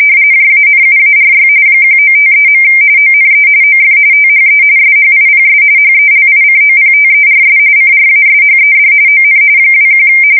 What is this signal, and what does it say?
RTTY, about 45.45 baud. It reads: JANUARY 1953. ONE OF THE SHIPS HELICOPTERS CRASHED ON 21
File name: fsk170 / USS-Missouri